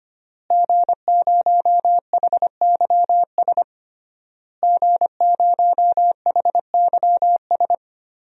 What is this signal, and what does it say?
Morse code, heard as a keyed tone, about 25 wpm, 700 Hz. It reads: G0HYS G0HYS
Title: DXCC-100 Call Signs - 5 Characters - Repeated 3x - 25wpm